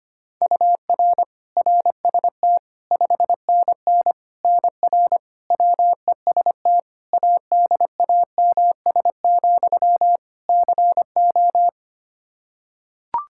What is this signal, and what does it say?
Morse code, heard as a keyed tone, about 25 wpm, 700 Hz. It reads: UR RST 5NN NR WEST ADAMS, CO
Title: Practice QSO - Exchange 1 - Beginning - 25wpm